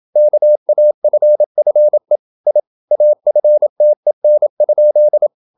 Morse code, heard as a keyed tone, about 28 wpm, 600 Hz. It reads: KAFFE I AFTEN?
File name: Ferie QRQ 27wpm